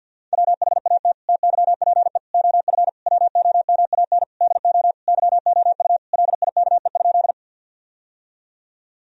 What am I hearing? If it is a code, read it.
Morse code, heard as a keyed tone, about 50 wpm, 700 Hz. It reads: WHAT TYPE OF WOMAN DO YOU LIKE?
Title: Sentences from Top 1000 Words - Review Part 3 - 50wpm